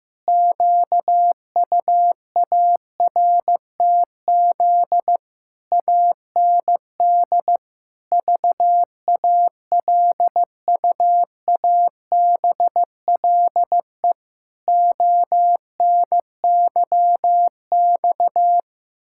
Morse code, heard as a keyed tone, about 16 wpm, 700 Hz. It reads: QUARTZ AND VALUABLE ONYX
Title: Pangram Challenge v2 - 15wpm